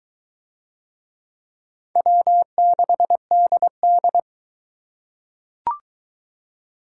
Morse code, heard as a keyed tone, about 22 wpm, 700 Hz. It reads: W6DD